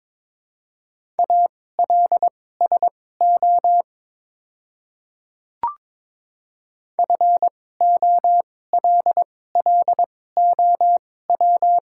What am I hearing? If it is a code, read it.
Morse code, heard as a keyed tone, about 22 wpm, spaced as for 16 wpm, 700 Hz. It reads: ALSO FOLLOW
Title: 2x Character Spacing - Top 200 Words - 22wpm